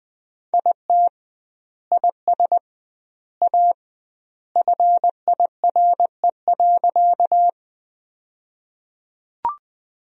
Morse code, heard as a keyed tone, about 20 wpm, 700 Hz. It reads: IT IS A FIRE.
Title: 2x Word-Spacing - Sentences from Top 400 Words - 20wpm